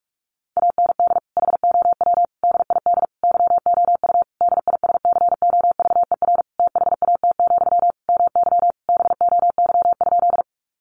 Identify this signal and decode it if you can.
Morse code, heard as a keyed tone, about 44 wpm, 700 Hz. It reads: AND HOW DID YOU DISCOVER THAT, MY BOY?
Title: Sentences from Top 1100 Words 45wpm